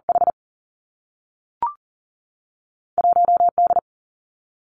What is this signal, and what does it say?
Morse code, heard as a keyed tone, about 39 wpm, 700 Hz. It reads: H 1 D
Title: Two Challenging Letter-Number Combinations 40wpm